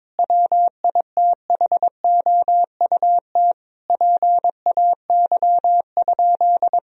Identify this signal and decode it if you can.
Morse code, heard as a keyed tone, about 22 wpm, 700 Hz. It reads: WITHOUT PAY?
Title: Sentences from Top 1000 Words - Review Part 3 - 22wpm